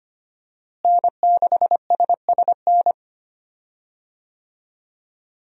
Morse code, heard as a keyed tone, about 25 wpm, 700 Hz. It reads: N6SSN
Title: USA Call Signs - 5 Characters - Repeated 3x - 25wpm